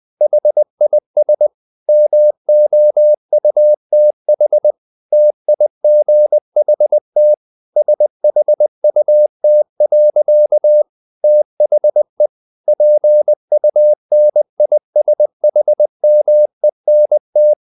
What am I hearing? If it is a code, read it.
Morse code, heard as a keyed tone, about 20 wpm, 600 Hz. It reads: HIS MOUTH TIGHT SHUT. THE PUNISHMENT
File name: the old man and the sea part 2 20wpm